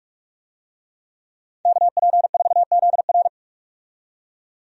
Morse code, heard as a keyed tone, about 46 wpm, 700 Hz. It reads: KP4ZR
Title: USA Call Signs - 5 Characters - 45wpm